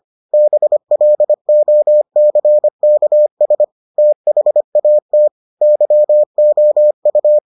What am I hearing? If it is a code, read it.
Morse code, heard as a keyed tone, about 25 wpm, 600 Hz. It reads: BLOCKS THAT YOU